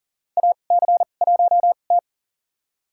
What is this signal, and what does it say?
Morse code, heard as a keyed tone, about 40 wpm, spaced as for 29 wpm, 700 Hz. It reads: AC1T
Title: USA Call Signs - 2x Character Spacing - 40wpm